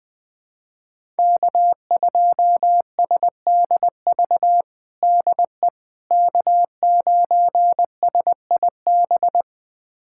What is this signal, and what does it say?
Morse code, heard as a keyed tone, about 21 wpm, 700 Hz. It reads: K2SDV DE K9SIB
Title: USA Callsign DE Callsign - 5 Characters - 20wpm